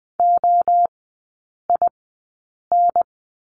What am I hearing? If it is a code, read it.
Morse code, heard as a keyed tone, about 20 wpm, spaced as for 8 wpm, 700 Hz. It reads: OIN
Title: 041 Lesson 8e - Number 1 - Three Characters 20wpm